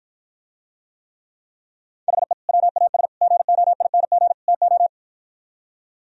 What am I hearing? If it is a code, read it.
Morse code, heard as a keyed tone, about 52 wpm, 700 Hz. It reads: HE WAS GOING TO